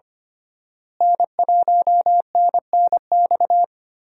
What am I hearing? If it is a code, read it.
Morse code, heard as a keyed tone, about 25 wpm, 700 Hz. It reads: N1NNX
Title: USA Call Signs - 5 Characters - Repeated 3x - 25wpm